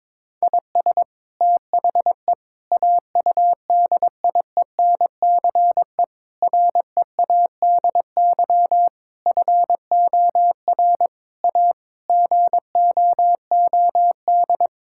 Morse code, heard as a keyed tone, about 22 wpm, 700 Hz. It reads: IS THE AUDIENCE READY FOR A GOOD
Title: Sentences from Top 1000 Words - Review Part 5 - 22wpm